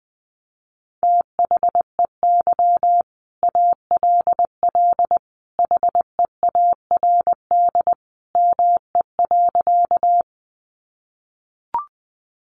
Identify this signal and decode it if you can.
Morse code, heard as a keyed tone, about 19 wpm, 700 Hz. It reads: THEY ALL HEARD ME.
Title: Sentences from Top 400 Words 20wpm